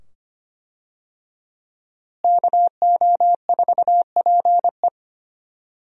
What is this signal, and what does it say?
Morse code, heard as a keyed tone, about 25 wpm, 700 Hz. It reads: KO4PE
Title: USA Call Signs - 5 Characters - Repeated 2x - 25wpm